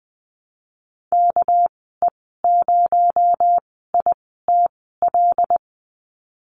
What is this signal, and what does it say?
Morse code, heard as a keyed tone, about 20 wpm, spaced as for 15 wpm, 700 Hz. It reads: KE0ITL
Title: 219 Lesson 35g - Number 0 - Call Signs 20wpm